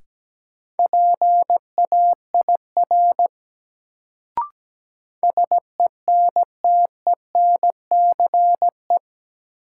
Morse code, heard as a keyed tone, about 17 wpm, 700 Hz. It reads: PAIR SENTENCE
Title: Sentences from Top 1200 Words - New Word Drill Alt - 17wpm